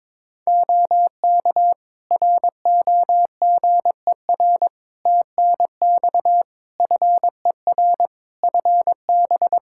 Morse code, heard as a keyed tone, about 22 wpm, 700 Hz. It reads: OK ROGER TNX FER FB
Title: Practice QSO - Exchange 4 - Ending - 22wpm